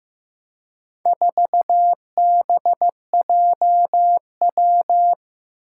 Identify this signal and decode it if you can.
Morse code, heard as a keyed tone, about 15 wpm, 700 Hz. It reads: 4BJW
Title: Four Challenging Letter-Number Combinations 15wpm